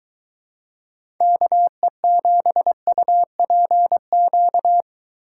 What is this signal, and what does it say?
Morse code, heard as a keyed tone, about 21 wpm, 700 Hz. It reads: KE7UPQ